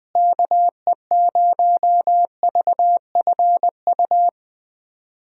Morse code, heard as a keyed tone, about 20 wpm, 700 Hz. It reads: KE0VFU
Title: USA Call Signs - 6 Characters - Repeated 2x - 20wpm